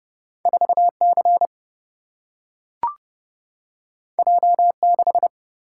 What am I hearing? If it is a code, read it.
Morse code, heard as a keyed tone, about 30 wpm, 700 Hz. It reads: 4 C J 6
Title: Single-Letter Single-Number 30wpm